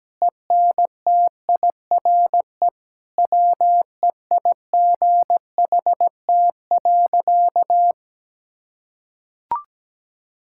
Morse code, heard as a keyed tone, about 17 wpm, 700 Hz. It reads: ENTIRE WEIGHT.
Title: Sentences from Top 1000 Words - Review Part 5 - 17wpm